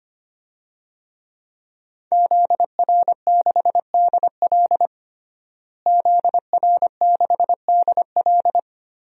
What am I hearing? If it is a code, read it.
Morse code, heard as a keyed tone, about 25 wpm, 700 Hz. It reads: ZR6DL ZR6DL
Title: DXCC-100 Call Signs - 5 Characters - Repeated 3x - 25wpm